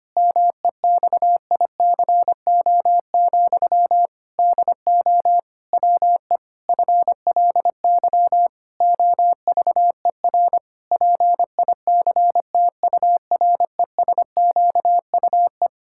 Morse code, heard as a keyed tone, about 25 wpm, 700 Hz. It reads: MEXICO, DO WE FLY OVER PICTURESQUE
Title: Pangram Challenge v2 - 25wpm